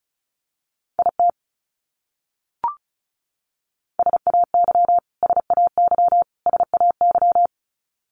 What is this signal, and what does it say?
Morse code, heard as a keyed tone, about 34 wpm, 700 Hz. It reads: IT SAY SAY SAY
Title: Top 300 Words Repeated 35wpm